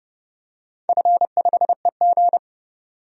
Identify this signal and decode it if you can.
Morse code, heard as a keyed tone, about 29 wpm, 700 Hz. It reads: F5EG
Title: DXCC-100 Call Signs - 4 Characters - 30wpm